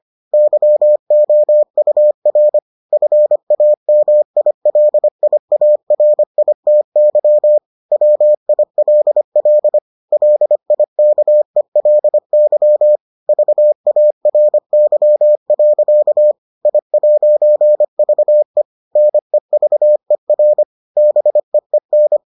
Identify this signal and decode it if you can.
Morse code, heard as a keyed tone, about 25 wpm, 600 Hz. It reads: YOUR FAMILIARITY WILL LIKELY VARY. I'VE NEVER BEEN